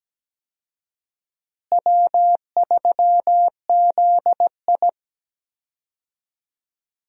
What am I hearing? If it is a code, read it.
Morse code, heard as a keyed tone, about 17 wpm, 700 Hz. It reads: W3ZI
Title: USA Call Signs - 4 Characters - Repeated 2x - 17wpm